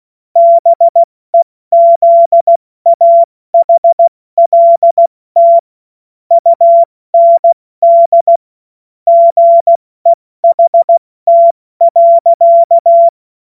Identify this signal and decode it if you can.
Morse code, heard as a keyed tone, about 16 wpm, 680 Hz. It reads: BEZAHLT UND GEHT.
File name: Morsekurs / Witze / Witz 05 März 2024 70 F80